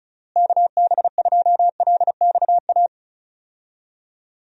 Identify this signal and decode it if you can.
Morse code, heard as a keyed tone, about 36 wpm, 700 Hz. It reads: KB2LXA
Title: USA Call Signs - 6 Characters - Repeated 3x - 35wpm